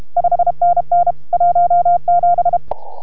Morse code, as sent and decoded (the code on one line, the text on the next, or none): ..... -. -. / .---- --...
5NN 17